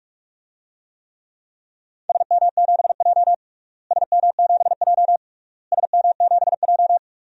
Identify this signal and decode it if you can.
Morse code, heard as a keyed tone, about 42 wpm, 700 Hz. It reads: SM7J SM7J SM7J
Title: DXCC-100 Call Signs - 4 Characters - Repeated 3x - 45wpm